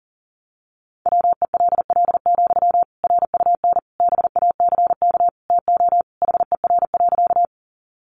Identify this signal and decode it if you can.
Morse code, heard as a keyed tone, about 40 wpm, 700 Hz. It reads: WELL, RUN BACK TO HER.
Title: Sentences from Top 300 Words 40wpm